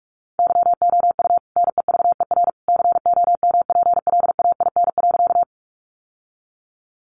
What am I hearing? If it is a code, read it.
Morse code, heard as a keyed tone, about 44 wpm, 700 Hz. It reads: YOU NEVER COMPLAIN.
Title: Sentences from Top 1300 Words 45wpm